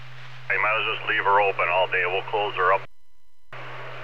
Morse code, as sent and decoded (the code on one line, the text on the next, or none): none